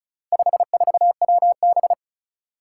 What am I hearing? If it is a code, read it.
Morse code, heard as a keyed tone, about 36 wpm, 700 Hz. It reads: 54WB
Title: Four Challenging Letter-Number Combinations 35wpm